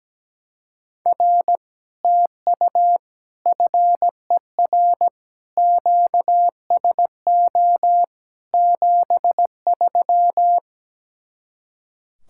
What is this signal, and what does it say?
Morse code, heard as a keyed tone, about 17 wpm, 700 Hz. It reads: R TU FER QSO 73
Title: Speed-Racing Common QSO Phrases - 17wpm